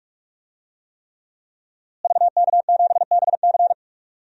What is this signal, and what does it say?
Morse code, heard as a keyed tone, about 45 wpm, 700 Hz. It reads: VK7BC
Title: DXCC-100 Call Signs - 5 Characters - Repeated 3x - 45wpm